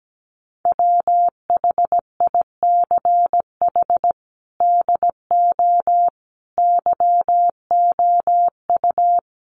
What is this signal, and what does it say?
Morse code, heard as a keyed tone, about 17 wpm, 700 Hz. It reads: WHICH DO YOU
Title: Sentences from Top 1100 Words 17wpm